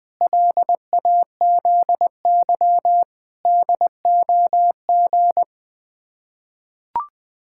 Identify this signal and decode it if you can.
Morse code, heard as a keyed tone, about 20 wpm, 700 Hz. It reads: LAZY DOG
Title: Sets of 8 Words - 20wpm